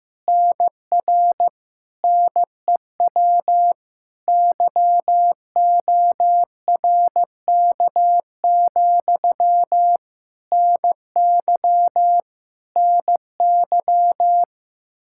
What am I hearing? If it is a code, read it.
Morse code, heard as a keyed tone, about 16 wpm, 700 Hz. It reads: NR NEW YORK, NY NY
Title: Practice QSO - Exchange 1 - Beginning - 15wpm